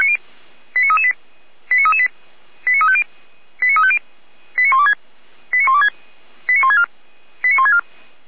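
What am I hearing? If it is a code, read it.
Five-tone selective call (ZVEI), heard as quick runs of five tones, tones of about 70 ms. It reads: 89338 89338 89350 89350 89126 89126 89164 89164
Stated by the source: ZVEI five-tone call 89401 received on SDR.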